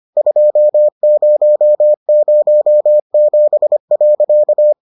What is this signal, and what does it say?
Morse code, heard as a keyed tone, about 25 wpm, 600 Hz. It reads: 2007.